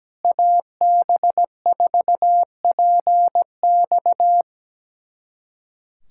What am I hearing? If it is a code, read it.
Morse code, heard as a keyed tone, about 17 wpm, 700 Hz. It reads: AB4PX
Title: USA Callsign DE Callsign - 5 Characters - 17wpm